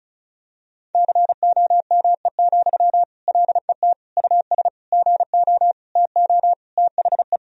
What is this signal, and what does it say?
Morse code, heard as a keyed tone, about 36 wpm, 700 Hz. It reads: COME, LET US GO TO THE
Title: Sentences from Top 1000 Words - Review Part 4 - 35wpm